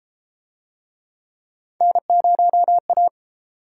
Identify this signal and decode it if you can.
Morse code, heard as a keyed tone, about 31 wpm, 700 Hz. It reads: N0A